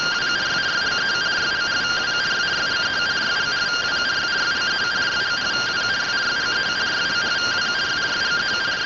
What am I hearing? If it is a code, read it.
Two-tone teleprinter signal, about 170 Hz shift.
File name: findings / AXIAL fr130 3mm 409.gen-35 rtty rx